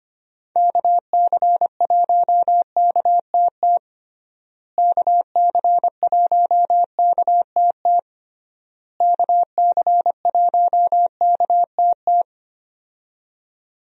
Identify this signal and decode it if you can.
Morse code, heard as a keyed tone, about 25 wpm, 700 Hz. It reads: KC1KTT KC1KTT KC1KTT
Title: USA Call Signs - 6 Characters - Repeated 3x - 25wpm